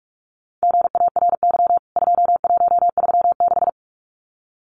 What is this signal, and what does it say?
Morse code, heard as a keyed tone, about 44 wpm, 700 Hz. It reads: GARY 2136
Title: CWT Exchange 45wpm